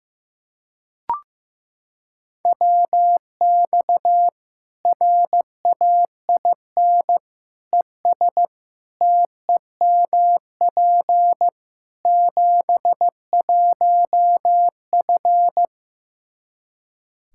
Morse code, heard as a keyed tone, about 16 wpm, 700 Hz. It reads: WX RAIN ES TEMP 71F
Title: K1USN SST QSO Practice 15wpm